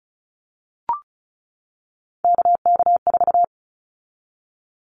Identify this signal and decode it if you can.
Morse code, heard as a keyed tone, about 36 wpm, 700 Hz. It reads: KK4
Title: Callsign Prefixes for Top 25 DXCC - 35wpm